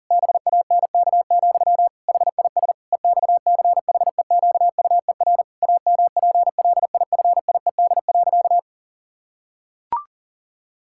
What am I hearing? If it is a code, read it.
Morse code, heard as a keyed tone, about 40 wpm, 700 Hz. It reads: BANK, HIS EXCHEQUER AMPLIFIED.
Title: Pangram Challenge v2 - 40wpm